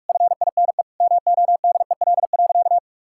Morse code, heard as a keyed tone, about 46 wpm, 700 Hz. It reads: FINE MODEL.
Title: Sentences from Top 1000 Words - Review Part 5 - 45wpm